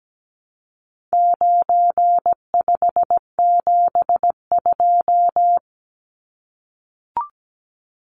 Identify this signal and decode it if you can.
Morse code, heard as a keyed tone, about 17 wpm, 700 Hz. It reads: 9572
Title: Numbers - 4-Digit 17wpm